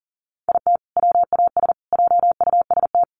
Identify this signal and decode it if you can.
Morse code, heard as a keyed tone, about 38 wpm, 700 Hz. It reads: IT WAS JUST
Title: Sets of 3 Words 40wpm